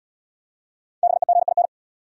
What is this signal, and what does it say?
Morse code, heard as a keyed tone, about 66 wpm, 700 Hz. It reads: BEGIN